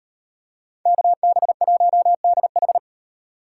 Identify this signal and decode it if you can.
Morse code, heard as a keyed tone, about 37 wpm, 700 Hz. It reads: KB1DH